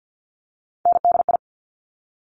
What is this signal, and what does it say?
Morse code, heard as a keyed tone, about 50 wpm, 700 Hz. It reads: NDI
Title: Top Three Letter Combinations in Words 50wpm